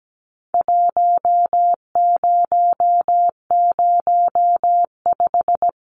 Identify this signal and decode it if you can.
Morse code, heard as a keyed tone, about 17 wpm, 700 Hz. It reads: 1005
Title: Numbers - Contest Style - 4-Digit 17wpm